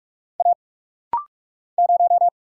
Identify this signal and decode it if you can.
Morse code, heard as a keyed tone, about 43 wpm, 700 Hz. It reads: A 0
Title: Single Letter-Number Alt - Rapid-Fire 45wpm